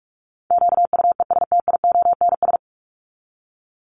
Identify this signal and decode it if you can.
Morse code, heard as a keyed tone, about 44 wpm, 700 Hz. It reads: QUESTIONS
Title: ES:S Suffix - Top 100 Words with Base Words 45wpm